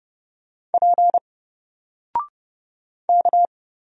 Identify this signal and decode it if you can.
Morse code, heard as a keyed tone, about 30 wpm, 700 Hz. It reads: P K
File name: Single-Letter-Number-30wpm